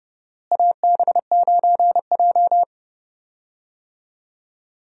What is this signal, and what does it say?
Morse code, heard as a keyed tone, about 30 wpm, 700 Hz. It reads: AB9J
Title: USA Call Signs - 4 Characters - Repeated 2x - 30wpm